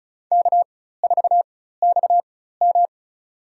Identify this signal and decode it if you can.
Morse code, heard as a keyed tone, about 36 wpm, spaced as for 16 wpm, 700 Hz. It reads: K4XM
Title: USA Call Signs - 4 Characters - 4x Character Spacing - 35wpm